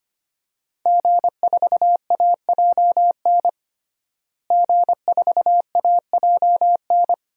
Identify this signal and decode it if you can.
Morse code, heard as a keyed tone, about 25 wpm, 700 Hz. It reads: G4AJN G4AJN
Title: DXCC-100 Call Signs - 5 Characters - Repeated 2x - 25wpm